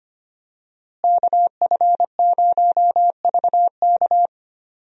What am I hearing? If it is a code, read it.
Morse code, heard as a keyed tone, about 25 wpm, 700 Hz. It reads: KF0VK
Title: USA Call Signs - 5 Characters - Repeated 3x - 25wpm